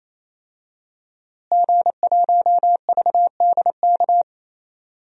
Morse code, heard as a keyed tone, about 27 wpm, 700 Hz. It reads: G1VDK
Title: DXCC-100 Call Signs - 5 Characters - Repeated 3x - 28wpm